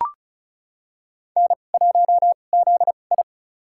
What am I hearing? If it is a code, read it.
Morse code, heard as a keyed tone, about 36 wpm, spaced as for 25 wpm, 700 Hz. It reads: N1ZI
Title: USA Call Signs - 2x Character Spacing - 35wpm